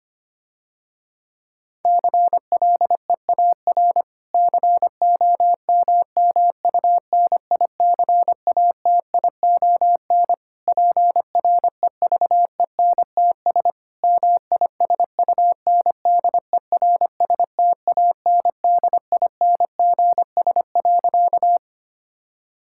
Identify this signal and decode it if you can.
Morse code, heard as a keyed tone, about 25 wpm, 700 Hz. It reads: CLEAR COMMUNICATION PREVENTS MISUNDERSTANDINGS.
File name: practice / 25wpm / Difficult Sentences with Hints - 25wpm